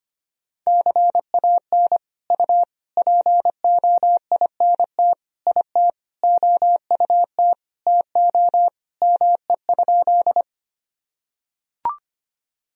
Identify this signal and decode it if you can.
Morse code, heard as a keyed tone, about 25 wpm, 700 Hz. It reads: CAN U POINT IT OUT TO ME?
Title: Sentences from Top 200 Words and Abbreviations 25wpm